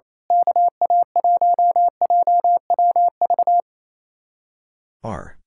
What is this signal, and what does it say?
Morse code, heard as a keyed tone, about 28 wpm, 700 Hz. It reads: KA1JWV
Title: Speed-Racing Common QSO Phrases - 28wpm